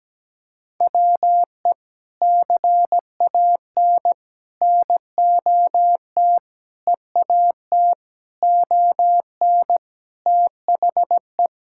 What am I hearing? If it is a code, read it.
Morse code, heard as a keyed tone, about 17 wpm, 700 Hz. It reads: WE CAN NOT EAT ON THE
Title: Sentences from Top 1000 Words - Review Part 6 - 17wpm